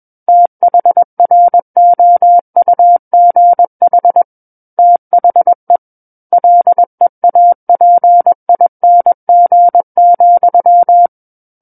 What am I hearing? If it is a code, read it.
Morse code, heard as a keyed tone, about 21 wpm, 700 Hz. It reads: THROUGH THE LEAPING,
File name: Book One / 12-Chapter 12 - 21 WPM